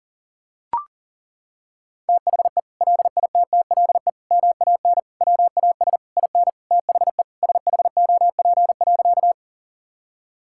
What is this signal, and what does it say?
Morse code, heard as a keyed tone, about 37 wpm, 700 Hz. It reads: THE LITTLE MAN WAS IN THE SHOP.
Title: Sentences from Top 1600 Words 40wpm